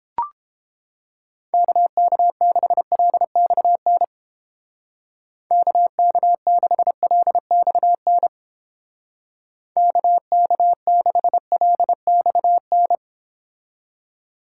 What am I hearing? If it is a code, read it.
Morse code, heard as a keyed tone, about 30 wpm, 700 Hz. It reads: KK6LXN KK6LXN KK6LXN